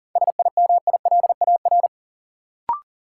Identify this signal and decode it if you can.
Morse code, heard as a keyed tone, about 39 wpm, 700 Hz. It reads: SIMILAR
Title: Sentences from Top 1000 Words - Review Part 5 - New Word Drill - 40wpm